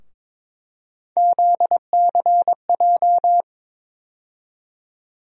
Morse code, heard as a keyed tone, about 21 wpm, 700 Hz. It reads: ZCJ
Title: USA Call Sign Postfixes - 3 Letters 22wpm